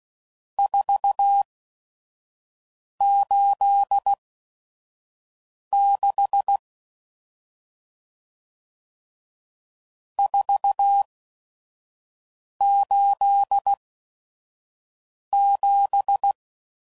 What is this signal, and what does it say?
Morse code, heard as a keyed tone, about 16 wpm, spaced as for 5 wpm, 800 Hz. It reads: 486 487
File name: CW Code Groups @5WPM / RST's / Various RST No's